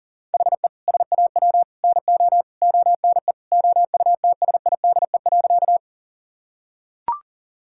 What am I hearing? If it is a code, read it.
Morse code, heard as a keyed tone, about 40 wpm, 690 Hz. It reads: HE SAW NO ONE OUTSIDE.
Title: Sentences from Top 1000 Words - Review Part 6 - 40wpm